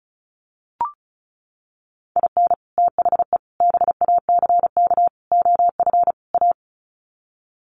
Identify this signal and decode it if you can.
Morse code, heard as a keyed tone, about 34 wpm, 700 Hz. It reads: IN THE BACK OF A
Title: Sets of 5 Words Encore2 35wpm